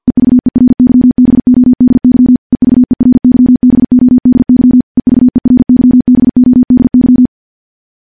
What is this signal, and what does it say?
Morse code, heard as a keyed tone, about 50 wpm, 250 Hz. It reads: EVERYBODY EVERYBODY EVERYBODY
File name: morse / morse-training-50WPM